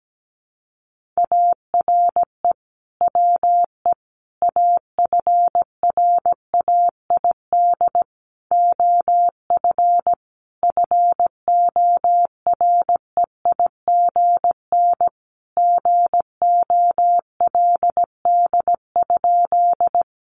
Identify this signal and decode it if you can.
Morse code, heard as a keyed tone, about 17 wpm, 700 Hz. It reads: ARE WE AFRAID OF FOREIGN GOLD?
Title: Sentences from Top 1500 Words 17wpm